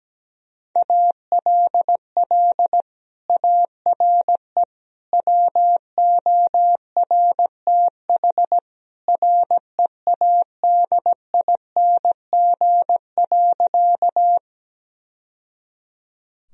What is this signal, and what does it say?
Morse code, heard as a keyed tone, about 17 wpm, 700 Hz. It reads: ALL ARE WORTH READING.
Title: Sentences from Top 1000 Words - Review Part 3 - 17wpm